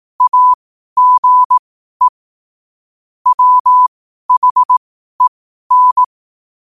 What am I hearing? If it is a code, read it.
Morse code, heard as a keyed tone, about 18 wpm, spaced as for 12 wpm, 1000 Hz. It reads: AGE WHEN
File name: qotdfiles / 12wpm